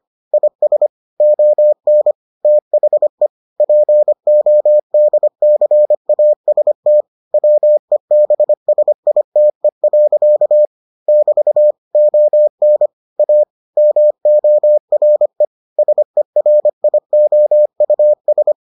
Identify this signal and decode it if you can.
Morse code, heard as a keyed tone, about 25 wpm, 600 Hz. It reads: IS ON THE PODCAST WEBSITE. <BT> ON A MORE SERIOUS